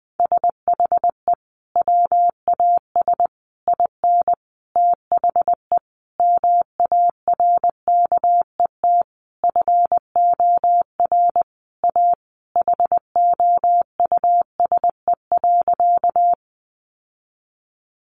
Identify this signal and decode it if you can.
Morse code, heard as a keyed tone, about 20 wpm, 700 Hz. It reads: SHE WAS IN THE MARKET FOR A HOUSE.
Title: Sentences from Top 600 Words 20wpm